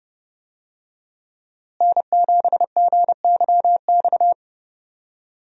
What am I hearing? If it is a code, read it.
Morse code, heard as a keyed tone, about 30 wpm, 700 Hz. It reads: N7GYX
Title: USA Call Signs - 5 Characters - Repeated 3x - 30wpm